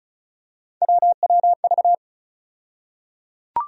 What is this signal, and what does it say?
Morse code, heard as a keyed tone, about 36 wpm, 700 Hz. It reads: WWV
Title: USA Call Sign Postfixes - 3 Letters 35wpm